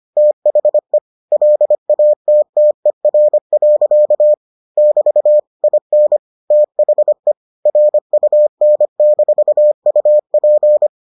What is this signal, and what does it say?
Morse code, heard as a keyed tone, about 25 wpm, 600 Hz. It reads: THE LATTER. <BT> IN THE RUN-UP